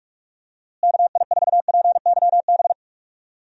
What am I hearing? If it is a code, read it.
Morse code, heard as a keyed tone, about 48 wpm, 700 Hz. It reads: KI4PYB
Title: USA Call Signs - 6 Characters - Repeated 3x - 45wpm